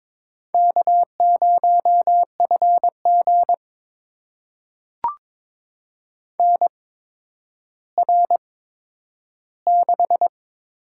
Morse code, heard as a keyed tone, about 22 wpm, spaced as for 11 wpm, 700 Hz. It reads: K0FG NR6
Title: USA Call Signs - 4 Characters - 8x Character Spacing - 22wpm